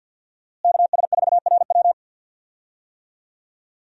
Morse code, heard as a keyed tone, about 50 wpm, 700 Hz. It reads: KS4RW
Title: USA Call Signs - 5 Characters - Repeated 3x - 50wpm